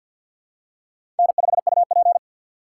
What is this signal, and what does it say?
Morse code, heard as a keyed tone, about 52 wpm, 700 Hz. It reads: N5UP
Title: USA Call Signs - 4 Characters - 2x Character Spacing - 50wpm